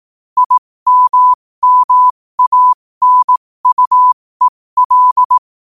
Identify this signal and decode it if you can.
Morse code, heard as a keyed tone, about 18 wpm, spaced as for 15 wpm, 1000 Hz. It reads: IMMANUEL
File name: qotdfiles / 15wpm